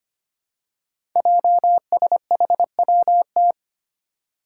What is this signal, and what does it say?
Morse code, heard as a keyed tone, about 25 wpm, 700 Hz. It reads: JSHWT
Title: Five Challenging Letter-Number Combinations 25wpm